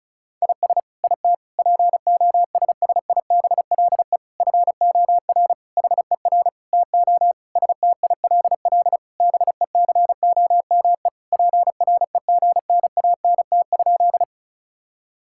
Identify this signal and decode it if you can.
Morse code, heard as a keyed tone, about 36 wpm, 700 Hz. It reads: IS IT POSSIBLE FOR HER TO STILL BECOME PREGNANT?
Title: Sentences from Top 1000 Words - Review Part 6 - 35wpm